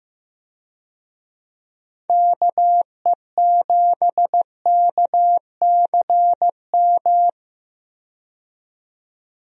Morse code, heard as a keyed tone, about 15 wpm, 700 Hz. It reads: KE7KCM
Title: USA Call Signs - 6 Characters - 15wpm